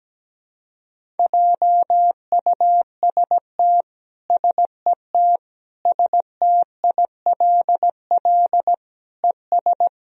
Morse code, heard as a keyed tone, about 17 wpm, 700 Hz. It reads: JUST SET STILL ES
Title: Sentences from Top 300 Words and Abbreviations 17wpm